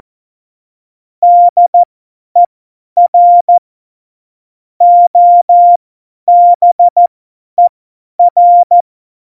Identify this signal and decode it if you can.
Morse code, heard as a keyed tone, about 14 wpm, spaced as for 10 wpm, 700 Hz. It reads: DER OBER
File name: Morsekurs / Witze / Witz 02. April 2019, 50 F70